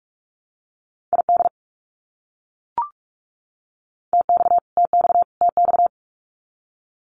Morse code, heard as a keyed tone, about 44 wpm, 700 Hz. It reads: ID TX TX TX
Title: US States Repeated 3x 45wpm